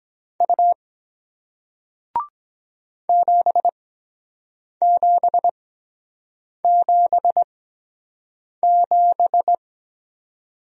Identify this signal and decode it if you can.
Morse code, heard as a keyed tone, about 23 wpm, spaced as for 7 wpm, 700 Hz. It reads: U 7777